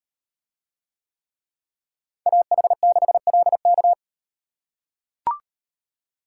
Morse code, heard as a keyed tone, about 38 wpm, 700 Hz. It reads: AH6LK